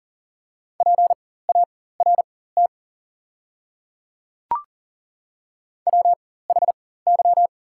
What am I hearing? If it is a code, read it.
Morse code, heard as a keyed tone, about 40 wpm, spaced as for 19 wpm, 690 Hz. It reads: PART WHY
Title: 4x Character Spacing - Top 200 Words - 40wpm